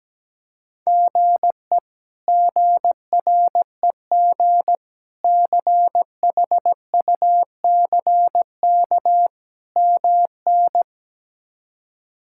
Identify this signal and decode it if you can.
Morse code, heard as a keyed tone, about 17 wpm, 700 Hz. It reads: GE GREG CHUCK MN
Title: K1USN SST QSO Practice - 17wpm